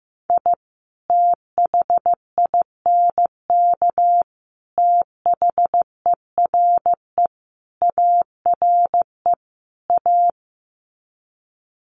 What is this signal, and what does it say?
Morse code, heard as a keyed tone, about 15 wpm, 700 Hz. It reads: I THINK THERE ARE A
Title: Sets of 5 Words Encore2 15wpm